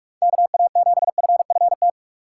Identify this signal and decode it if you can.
Morse code, heard as a keyed tone, about 44 wpm, 700 Hz. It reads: KA7FFT
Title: USA Call Signs - 6x Character Spacing - 45wpm